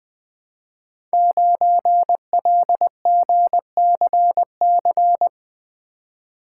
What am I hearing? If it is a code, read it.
Morse code, heard as a keyed tone, about 20 wpm, 700 Hz. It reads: 9LGCC
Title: Code Groups - 5 Letter:Numbers - 20wpm